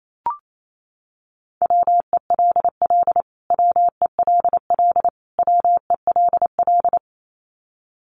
Morse code, heard as a keyed tone, about 28 wpm, 700 Hz. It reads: WELL WELL WELL
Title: Top 300 Words Repeated 28wpm